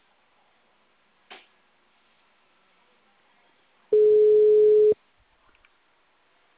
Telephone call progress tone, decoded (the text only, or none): ringback tone